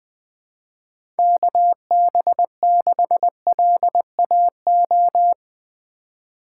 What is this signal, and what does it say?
Morse code, heard as a keyed tone, about 20 wpm, 700 Hz. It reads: KB6LAO
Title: USA Call Signs - 6 Characters - 2x Character Spacing - 20wpm